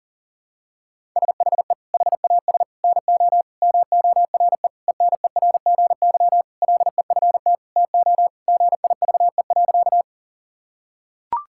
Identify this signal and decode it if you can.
Morse code, heard as a keyed tone, about 40 wpm, 700 Hz. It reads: SHE HAS NO MORE ENERGY LEFT TO GIVE.
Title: Sentences from Top 1000 Words - Review Part 6 - 40wpm